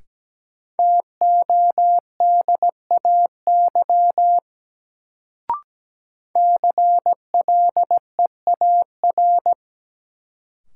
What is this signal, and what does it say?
Morse code, heard as a keyed tone, about 17 wpm, 700 Hz. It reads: TODAY CLEAR
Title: Sets of 5 Words Encore - New Word Drill - 17wpm